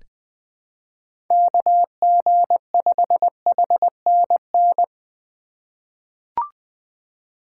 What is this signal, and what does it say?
Morse code, heard as a keyed tone, about 20 wpm, 700 Hz. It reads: KG5HNN
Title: USA Call Signs - 3x Character Spacing - 20wpm